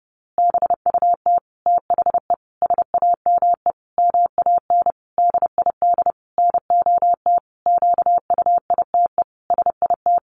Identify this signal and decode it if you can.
Morse code, heard as a keyed tone, about 30 wpm, 700 Hz. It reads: BUT THE SAME MAN DID NOT QUITE SIT
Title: A Sound Of Thunder 30wpm